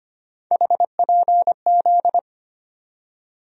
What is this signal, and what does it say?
Morse code, heard as a keyed tone, about 25 wpm, 700 Hz. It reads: HPZ
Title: USA Call Sign Postfixes - 3 Letters 25wpm